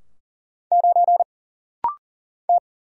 Morse code, heard as a keyed tone, about 39 wpm, 700 Hz. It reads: 9 T
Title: Single Letter-Number Alt - Rapid-Fire 40wpm